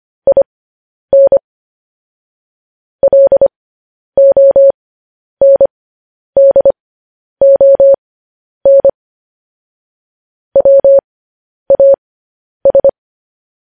Morse code, heard as a keyed tone, about 25 wpm, spaced as for 10 wpm, 550 Hz. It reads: IN LONDON WAS
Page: news-headlines-in-morse-code-at-10-wpm